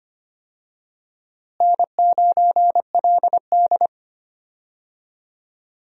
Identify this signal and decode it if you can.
Morse code, heard as a keyed tone, about 25 wpm, 700 Hz. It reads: N9LD
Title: USA Call Signs - 4 Characters - Repeated 3x - 25wpm